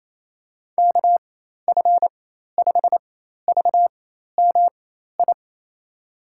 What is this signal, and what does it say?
Morse code, heard as a keyed tone, about 28 wpm, spaced as for 13 wpm, 700 Hz. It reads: KF5VMI
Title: USA Call Signs - 6 Characters - 4x Character Spacing - 28wpm